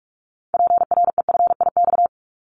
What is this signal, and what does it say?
Morse code, heard as a keyed tone, about 44 wpm, 700 Hz. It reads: PREFIX
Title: Callsign Prefixes for Top 100 DXCC - 45wpm